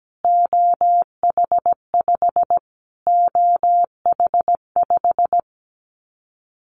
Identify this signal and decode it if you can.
Morse code, heard as a keyed tone, about 17 wpm, 700 Hz. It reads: OH5 OH5
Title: Callsign Prefixes for Top 25 DXCC - Repeated 3x - 17wpm